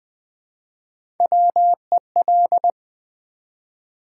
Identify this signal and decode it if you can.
Morse code, heard as a keyed tone, about 20 wpm, 700 Hz. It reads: WEL
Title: Build A Word - 1 Letter At A Time - 4-Letter Words - 20wpm